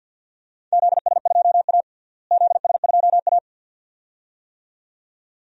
Morse code, heard as a keyed tone, about 52 wpm, 700 Hz. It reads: ZS2A ZS2A
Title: DXCC-100 Call Signs - 4 Characters - Repeated 2x - 50wpm